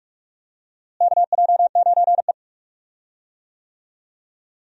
Morse code, heard as a keyed tone, about 46 wpm, 700 Hz. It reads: KJ9E
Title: USA Call Signs - 4 Characters - Repeated 3x - 45wpm